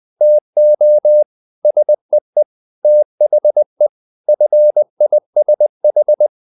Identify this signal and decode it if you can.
Morse code, heard as a keyed tone, about 20 wpm, 600 Hz. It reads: TO SEE THE FISH
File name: the old man and the sea part 2 20wpm